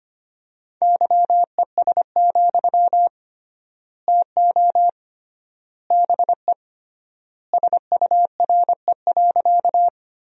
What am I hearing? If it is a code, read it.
Morse code, heard as a keyed tone, about 25 wpm, 700 Hz. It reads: YES, TO BE SURE.
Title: 3x Word-Spacing - Sentences from Top 500 Words - 25wpm